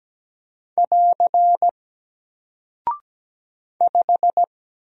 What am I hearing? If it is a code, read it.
Morse code, heard as a keyed tone, about 17 wpm, 700 Hz. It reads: <AR> 5
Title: Single Character (Letter:Number:Punctuation:Prosign) Alt 17wpm